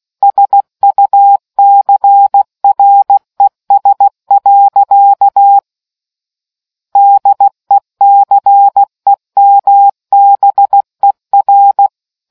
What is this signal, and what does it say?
Morse code, heard as a keyed tone, about 16 wpm, 800 Hz. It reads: SUCRES. DECEMBER